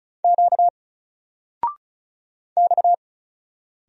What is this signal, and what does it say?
Morse code, heard as a keyed tone, about 36 wpm, 700 Hz. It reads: Q X
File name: Single-Letter-Number-35wpm